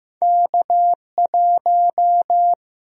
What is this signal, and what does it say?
Morse code, heard as a keyed tone, about 15 wpm, 700 Hz. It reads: K1
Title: USA Call Sign Prefixes - 15wpm